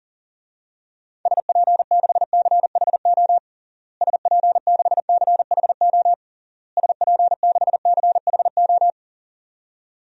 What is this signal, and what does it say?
Morse code, heard as a keyed tone, about 40 wpm, 700 Hz. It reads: SP6CHO SP6CHO SP6CHO
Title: DXCC-100 Call Signs - 6 Characters - Repeated 3x - 40wpm